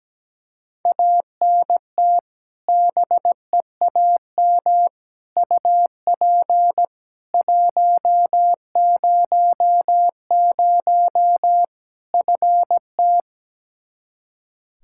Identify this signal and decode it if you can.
Morse code, heard as a keyed tone, about 17 wpm, 700 Hz. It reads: ANT BEAM UP 100 FT
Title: K1USN SST QSO Practice 17wpm